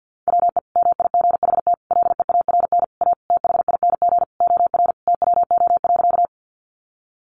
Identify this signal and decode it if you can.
Morse code, heard as a keyed tone, about 50 wpm, 700 Hz. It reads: WE MIGHT LEARN A THING OR TWO.
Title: Sentences from Top 300 Words 50wpm